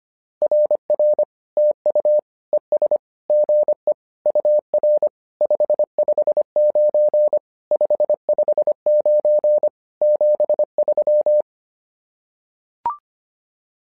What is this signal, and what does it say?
Morse code, heard as a keyed tone, about 25 wpm, 600 Hz. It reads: RR TU ES GE UR 559 559 73
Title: SOTA QSO Practice 25wpm